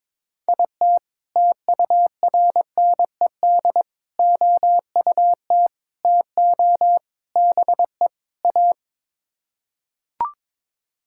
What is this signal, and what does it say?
Morse code, heard as a keyed tone, about 22 wpm, 700 Hz. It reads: IT TURNED OUT TO BE A
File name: practice / 22wpm / Sets of 6 Words - 22wpm